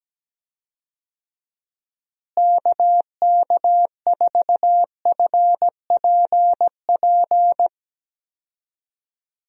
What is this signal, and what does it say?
Morse code, heard as a keyed tone, about 17 wpm, 700 Hz. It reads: KK4FPP
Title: USA Call Signs - 6 Characters - Repeated 3x - 17wpm